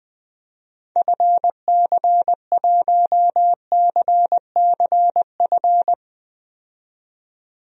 Morse code, heard as a keyed tone, about 20 wpm, 700 Hz. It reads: FC1CCF
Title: DXCC-100 Call Signs - 6 Characters - 20wpm